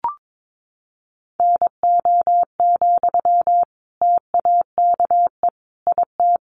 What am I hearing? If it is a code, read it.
Morse code, heard as a keyed tone, about 22 wpm, 700 Hz. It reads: NO, TAKE IT
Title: Sentences from Top 600 Words 22wpm